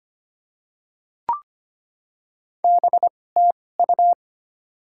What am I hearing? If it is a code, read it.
Morse code, heard as a keyed tone, about 25 wpm, spaced as for 18 wpm, 700 Hz. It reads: BTU
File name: Top 20 QSO Elements - 2x Character Spacing - 25wpm